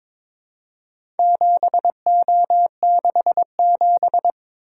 Morse code, heard as a keyed tone, about 22 wpm, 700 Hz. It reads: 7O67
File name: practice / 22wpm / Four Letter-Number Combinations - 22wpm